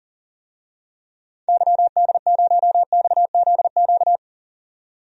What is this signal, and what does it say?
Morse code, heard as a keyed tone, about 40 wpm, 700 Hz. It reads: YD0XZQ
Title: DXCC-100 Call Signs - 6 Characters - 40wpm